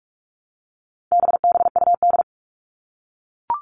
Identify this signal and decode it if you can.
Morse code, heard as a keyed tone, about 44 wpm, 700 Hz. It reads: BBUD
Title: All Cut Numbers - 4-Digit 45wpm